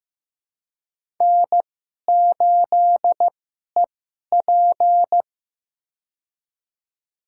Morse code, heard as a keyed tone, about 15 wpm, spaced as for 11 wpm, 700 Hz. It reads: N8EP
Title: USA Call Signs - 2x Character Spacing - 15wpm